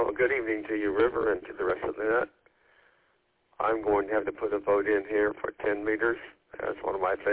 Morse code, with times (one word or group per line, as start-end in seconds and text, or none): none